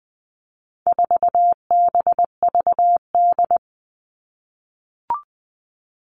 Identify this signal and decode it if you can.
Morse code, heard as a keyed tone, about 20 wpm, 700 Hz. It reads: 4BVD
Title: All Cut Numbers - 4-Digit 20wpm